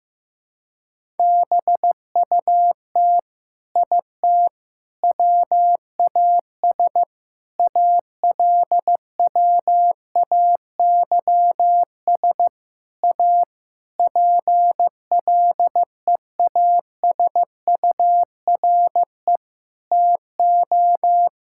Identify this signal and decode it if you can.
Morse code, heard as a keyed tone, about 15 wpm, 700 Hz. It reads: BUT IT WAS ALWAYS A PLEASURE TO
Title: Sentences from Top 1600 Words 15wpm